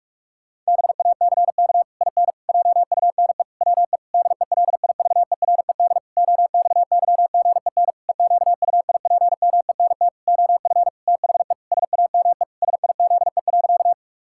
Morse code, heard as a keyed tone, about 46 wpm, 700 Hz. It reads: BACK IN JUNE WE DELIVERED OXYGEN EQUIPMENT OF THE SAME SIZE.
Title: Pangram Challenge v2 - 45wpm